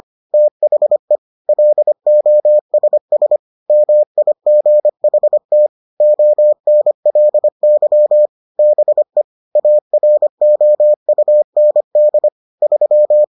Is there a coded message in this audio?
Yes: Morse code from 0.3 to 13.2 s, heard as a keyed tone, about 25 wpm, 600 Hz. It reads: THE LOSS MIGHT ONLY BE AROUND 3